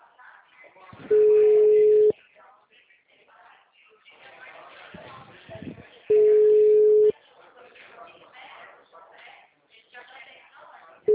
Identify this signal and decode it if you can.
Ringback tone (Europe, 425 Hz), with speech.